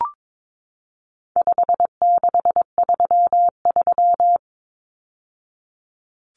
Morse code, heard as a keyed tone, about 22 wpm, 700 Hz. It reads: 5633
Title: Numbers - 4-Digit 22wpm